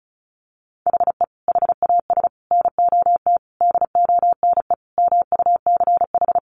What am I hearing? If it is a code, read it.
Morse code, heard as a keyed tone, about 34 wpm, 700 Hz. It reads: HE HAS NOT DONE MUCH
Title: Sentences from Top 1100 Words 35wpm